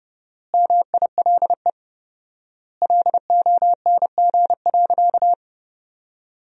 Morse code, heard as a keyed tone, about 30 wpm, 700 Hz. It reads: MILE LONG.
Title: 4x Word-Spacing - Sentences from Top 500 Words - 30wpm